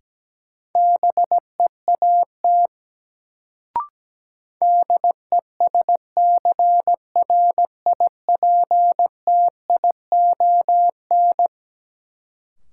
Morse code, heard as a keyed tone, about 17 wpm, 700 Hz. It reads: BEAT DESCRIPTION
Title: Sentences from Top 1300 Words - New Word Drill - 17wpm